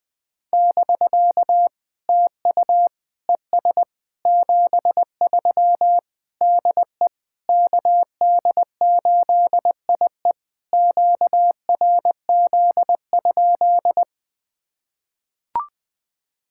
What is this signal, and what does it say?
Morse code, heard as a keyed tone, about 20 wpm, 700 Hz. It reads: <BK> TU ES 73 DE KD8IE QRZ?
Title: POTA QSO Practice - 20wpm